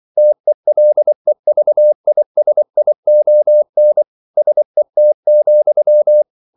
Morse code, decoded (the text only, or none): TELEVISION SET,